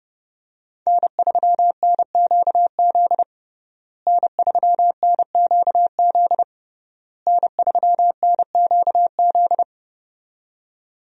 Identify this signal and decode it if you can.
Morse code, heard as a keyed tone, about 28 wpm, 700 Hz. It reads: N3NQZ N3NQZ N3NQZ
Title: USA Call Signs - 5 Characters - Repeated 3x - 30wpm